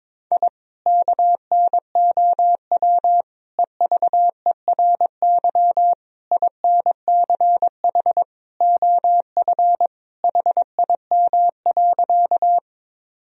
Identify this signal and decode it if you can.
Morse code, heard as a keyed tone, about 22 wpm, 700 Hz. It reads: I KNOW EVERY INCH OF HIM.
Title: Sentences from Top 1000 Words - Review Part 3 - 22wpm